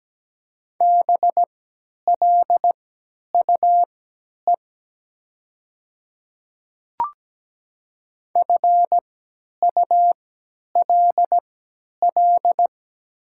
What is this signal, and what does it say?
Morse code, heard as a keyed tone, about 17 wpm, spaced as for 10 wpm, 700 Hz. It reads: BLUE FULL
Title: 3x Character Spacing - Top 500 Words - 17wpm